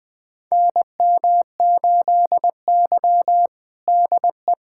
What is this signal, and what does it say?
Morse code, heard as a keyed tone, about 21 wpm, 700 Hz. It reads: NM8Y DE
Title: USA Callsign DE Callsign - 4 Characters - 20wpm